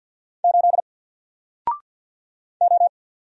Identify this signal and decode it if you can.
Morse code, heard as a keyed tone, about 48 wpm, 700 Hz. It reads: 8 O
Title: Single Character (Letter:Number:Punctuation:Prosign) 50wpm